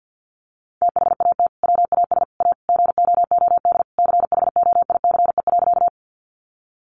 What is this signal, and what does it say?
Morse code, heard as a keyed tone, about 50 wpm, 700 Hz. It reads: THAT WAS A GOOD CHOICE.
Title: Sentences from Top 900 Words 50wpm